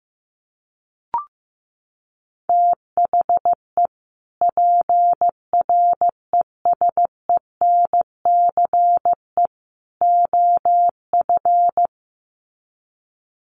Morse code, heard as a keyed tone, about 15 wpm, 700 Hz. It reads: THE PRESENCE OF
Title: Sets of 3 Words 15wpm